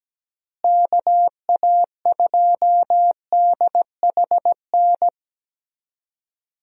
Morse code, heard as a keyed tone, about 17 wpm, 700 Hz. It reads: KA2DHN
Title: USA Call Signs - 6 Characters - Repeated 2x - 17wpm